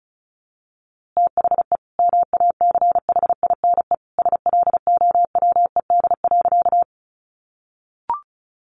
Morse code, heard as a keyed tone, about 34 wpm, 700 Hz. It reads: THE MACHINE SLOWED.
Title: A Sound Of Thunder 35wpm